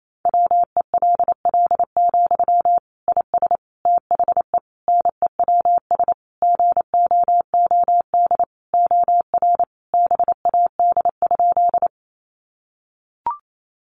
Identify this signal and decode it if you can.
Morse code, heard as a keyed tone, about 28 wpm, 700 Hz. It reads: WELL, IS THE NEWS GOOD OR BAD?
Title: Sentences from Top 800 Words 28wpm